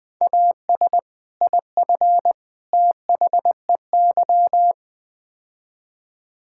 Morse code, heard as a keyed tone, about 20 wpm, 700 Hz. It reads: AS IF THEY
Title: Build A Phrase - 4-Word Phrases - 20wpm